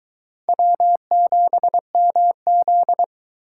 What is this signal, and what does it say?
Morse code, heard as a keyed tone, about 23 wpm, 700 Hz. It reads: W7MZ